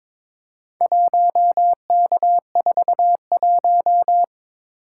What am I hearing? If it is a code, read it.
Morse code, heard as a keyed tone, about 22 wpm, 700 Hz. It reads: 1K41
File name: practice / 22wpm / Four Letter-Number Combinations - 22wpm